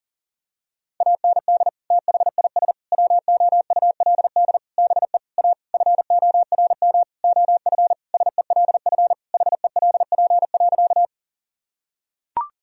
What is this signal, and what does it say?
Morse code, heard as a keyed tone, about 40 wpm, 700 Hz. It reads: AND THIS WOULD BE A FORM OF SELF HELP.
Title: Sentences from Top 1000 Words - Review Part 1 - 40wpm